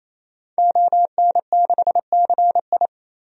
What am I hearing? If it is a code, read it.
Morse code, heard as a keyed tone, about 28 wpm, 700 Hz. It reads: ON6CI
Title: DXCC-100 Call Signs - 5 Characters - 28wpm